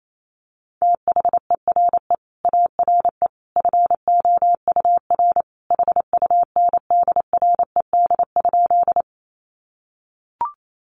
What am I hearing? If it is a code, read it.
Morse code, heard as a keyed tone, about 28 wpm, 700 Hz. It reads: THERE ARE FOUR HUNDRED?
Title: Sentences from Top 400 Words 28wpm